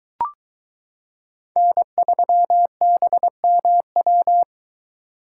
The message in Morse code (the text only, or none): N3BMW